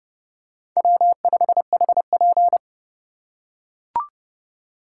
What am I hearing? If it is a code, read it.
Morse code, heard as a keyed tone, about 28 wpm, 700 Hz. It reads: W5HP